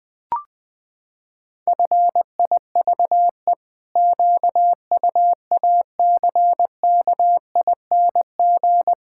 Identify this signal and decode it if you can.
Morse code, heard as a keyed tone, about 21 wpm, 700 Hz. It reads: FIVE QUACKING
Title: Pangram Challenge v2 - 20wpm